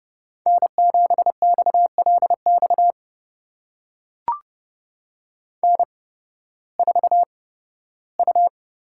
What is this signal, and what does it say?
Morse code, heard as a keyed tone, about 31 wpm, spaced as for 16 wpm, 700 Hz. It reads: N7XLX N4U
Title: USA Call Signs - 5 Characters - 8x Character Spacing - 30wpm